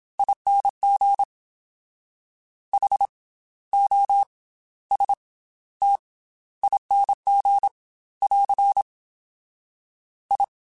Morse code, heard as a keyed tone, about 27 wpm, spaced as for 15 wpm, 800 Hz. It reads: ING HOSTING <AR> I
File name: Special INGs / Special INGs-mp3s / Slow ing @25